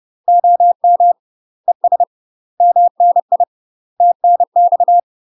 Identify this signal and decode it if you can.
Morse code, heard as a keyed tone, about 30 wpm, 700 Hz. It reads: OM ES MNI TNX
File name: UK MASTER QSOs TRAINING/30WPM